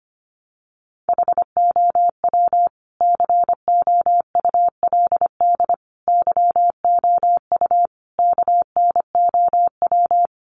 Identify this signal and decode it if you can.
Morse code, heard as a keyed tone, about 25 wpm, 700 Hz. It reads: HOW COULD YOU KNOW
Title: Sentences from Top 1000 Words 25wpm